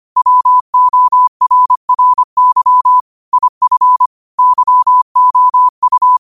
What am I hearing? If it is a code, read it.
Morse code, heard as a keyed tone, about 25 wpm, 1000 Hz. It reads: WORRY IF YOU
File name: qotdfiles / 25wpm